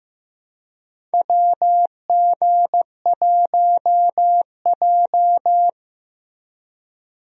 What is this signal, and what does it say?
Morse code, heard as a keyed tone, about 15 wpm, 700 Hz. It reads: WG1J
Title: USA Call Signs - 4 Characters - Repeated 3x - 15wpm